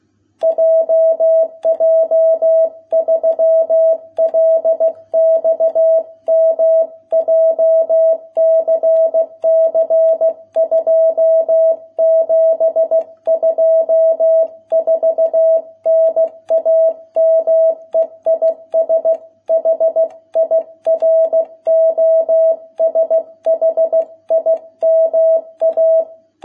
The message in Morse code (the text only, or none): JJ3LXM JCC 2724 NAME IS HIROSHIMA